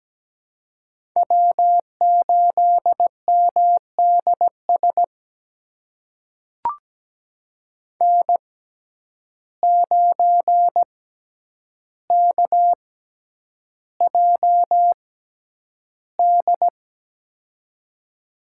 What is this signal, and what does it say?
Morse code, heard as a keyed tone, about 17 wpm, spaced as for 9 wpm, 700 Hz. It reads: W8MDS N9KJD
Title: USA Call Signs - 5 Characters - 6x Character Spacing - 17wpm